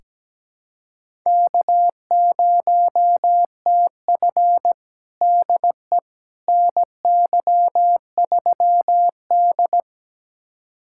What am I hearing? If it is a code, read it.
Morse code, heard as a keyed tone, about 17 wpm, 700 Hz. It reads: K0TF DE NY3D
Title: USA Callsign DE Callsign - 4 Characters - 17wpm